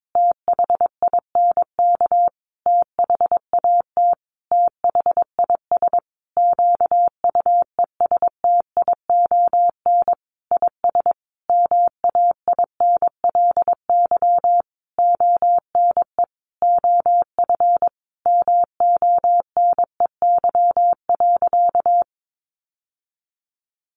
Morse code, heard as a keyed tone, about 22 wpm, 700 Hz. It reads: THINK THAT THIS QUESTION IS MAINLY ONE OF MONEY.
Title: Sentences from Top 1400 Words 22wpm